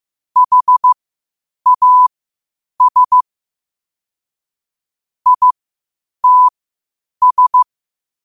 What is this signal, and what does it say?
Morse code, heard as a keyed tone, about 15 wpm, spaced as for 8 wpm, 1000 Hz. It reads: HAS ITS
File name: qotdfiles / 8wpm